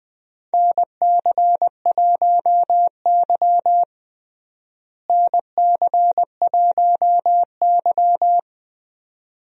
Morse code, heard as a keyed tone, about 20 wpm, 700 Hz. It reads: NC1Y NC1Y
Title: USA Call Signs - 4 Characters - Repeated 2x - 20wpm